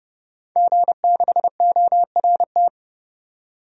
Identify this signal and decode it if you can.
Morse code, heard as a keyed tone, about 30 wpm, 700 Hz. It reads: G6ORT
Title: DXCC-100 Call Signs - 5 Characters - 30wpm